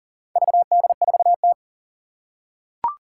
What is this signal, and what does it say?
Morse code, heard as a keyed tone, about 40 wpm, 700 Hz. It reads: VD4T
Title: Four Challenging Letter-Number Combinations 40wpm